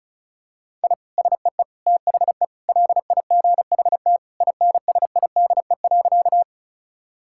Morse code, heard as a keyed tone, about 34 wpm, 700 Hz. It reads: I SEE THE LIGHT INSIDE.
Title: Sentences from Top 1000 Words - Review Part 6 - 35wpm